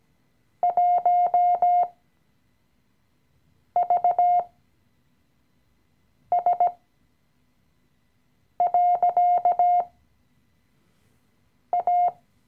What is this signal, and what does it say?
Morse code, heard as a keyed tone, about 18 wpm, spaced as for 4 wpm, 710 Hz. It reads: 1VS.A
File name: CW-Online-Kurs / Anfänger / Lektion 016